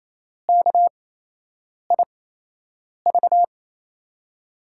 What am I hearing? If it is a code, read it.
Morse code, heard as a keyed tone, about 28 wpm, spaced as for 8 wpm, 700 Hz. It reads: KIV
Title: USA Call Signs - 8x Character Spacing - 28wpm